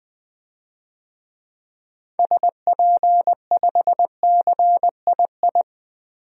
Morse code, heard as a keyed tone, about 20 wpm, 700 Hz. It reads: SP5CII
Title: DXCC-100 Call Signs - 6 Characters - 20wpm